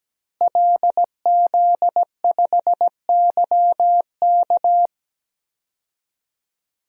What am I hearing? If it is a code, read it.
Morse code, heard as a keyed tone, about 17 wpm, 700 Hz. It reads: LZ5YK
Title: DXCC-100 Call Signs - 5 Characters - Repeated 2x - 17wpm